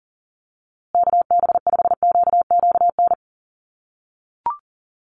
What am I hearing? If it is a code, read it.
Morse code, heard as a keyed tone, about 40 wpm, 700 Hz. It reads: KB5QQN
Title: USA Call Signs Repeated 3x 40wpm